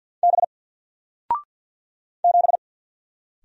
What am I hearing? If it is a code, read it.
Morse code, heard as a keyed tone, about 49 wpm, 700 Hz. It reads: B 7
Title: Single Letter-Number 50wpm